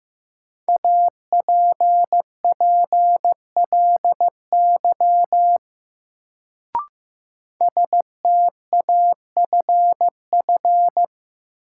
Morse code, heard as a keyed tone, about 15 wpm, 700 Hz. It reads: APPLY STAFF
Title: Sentences from Top 1200 Words - New Word Drill Alt - 15wpm